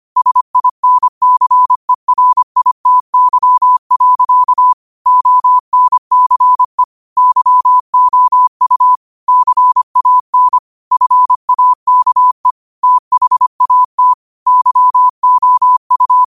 Morse code, heard as a keyed tone, about 25 wpm, 1000 Hz. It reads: SINCERITY. ONCE YOU CAN FAKE THAT YOU
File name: qotdfiles / 25wpm